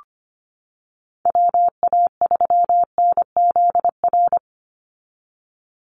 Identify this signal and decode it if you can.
Morse code, heard as a keyed tone, about 25 wpm, 700 Hz. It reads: WA3NZR
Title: Call Signs - 25wpm